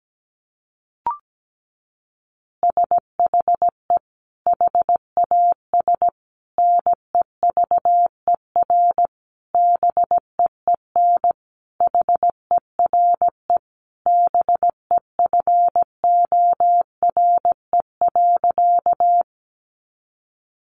Morse code, heard as a keyed tone, about 17 wpm, 700 Hz. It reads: SHE HAS NEVER BEEN HERE BEFORE.
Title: Additional Easy Phrases 17wpm